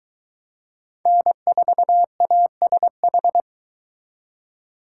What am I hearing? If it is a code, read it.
Morse code, heard as a keyed tone, about 22 wpm, 700 Hz. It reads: N4ASH